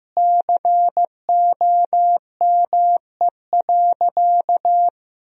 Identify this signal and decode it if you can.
Morse code, heard as a keyed tone, about 15 wpm, 700 Hz. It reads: COME.
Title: Sentences from Top 1000 Words - Review Part 4 - 15wpm